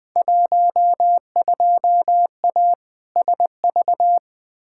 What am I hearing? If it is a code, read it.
Morse code, heard as a keyed tone, about 20 wpm, 700 Hz. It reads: 12A SV
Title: Field Day Exchanges Encore - 20wpm